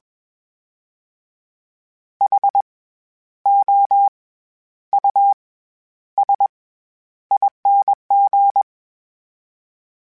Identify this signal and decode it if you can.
Morse code, heard as a keyed tone, about 21 wpm, spaced as for 11 wpm, 800 Hz. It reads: HOUSING
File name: Special INGs-mp3s / Special INGs-mp3s / SlowToFaster ing @20